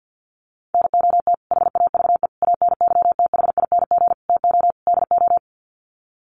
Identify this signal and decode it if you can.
Morse code, heard as a keyed tone, about 50 wpm, 700 Hz. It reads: NOT HAVE ANYTHING TO DO
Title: Sets of 5 Words Encore 50wpm